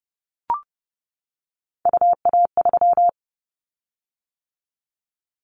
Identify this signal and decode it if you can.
Morse code, heard as a keyed tone, about 30 wpm, 700 Hz. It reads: UA3
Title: Callsign Prefixes for Top 25 DXCC - 30wpm